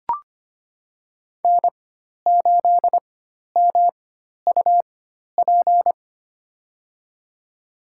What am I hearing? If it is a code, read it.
Morse code, heard as a keyed tone, about 25 wpm, spaced as for 12 wpm, 700 Hz. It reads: N8MUP
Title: USA Call Signs - 5 Characters - 4x Character Spacing - 25wpm